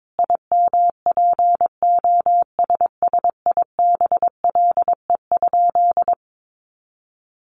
Morse code, heard as a keyed tone, about 22 wpm, 700 Hz. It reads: IMPOSSIBLE?
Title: Sentences from Top 1100 Words 22wpm